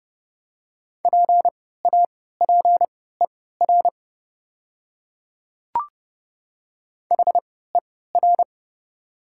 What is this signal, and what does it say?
Morse code, heard as a keyed tone, about 30 wpm, spaced as for 17 wpm, 700 Hz. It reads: PAPER HER
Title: 3x Character Spacing - Top 500 Words - 30wpm